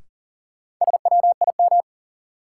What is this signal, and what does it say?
Morse code, heard as a keyed tone, about 40 wpm, 700 Hz. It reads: SWIM
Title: Sentences from Top 1000 Words - Review Part 6 - New Word Drill - 40wpm